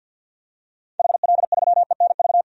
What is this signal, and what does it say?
Morse code, heard as a keyed tone, about 51 wpm, 700 Hz. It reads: HL3ENV
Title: DXCC-100 Call Signs - 6 Characters - 50wpm